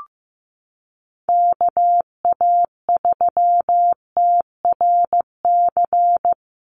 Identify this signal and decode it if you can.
Morse code, heard as a keyed tone, about 15 wpm, 700 Hz. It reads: KA3TRC
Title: USA Call Signs Repeated 3x 15wpm